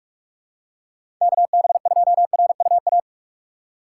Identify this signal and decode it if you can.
Morse code, heard as a keyed tone, about 42 wpm, 700 Hz. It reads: KB2RUA
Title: USA Call Signs - 6 Characters - 4x Character Spacing - 45wpm